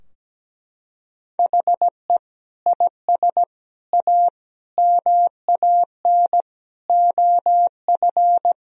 Morse code, heard as a keyed tone, about 17 wpm, 700 Hz. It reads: HE IS A MAN OF
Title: Sentences from Top 1000 Words - Review Part 2 - 17wpm